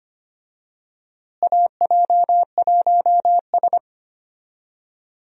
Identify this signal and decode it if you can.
Morse code, heard as a keyed tone, about 25 wpm, 700 Hz. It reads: AJ1S
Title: USA Call Signs - 4 Characters - Repeated 2x - 25wpm